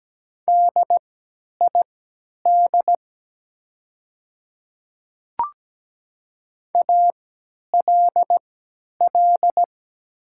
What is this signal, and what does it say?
Morse code, heard as a keyed tone, about 17 wpm, spaced as for 10 wpm, 700 Hz. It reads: DID ALL
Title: 3x Character Spacing - Top 100 Words - 17wpm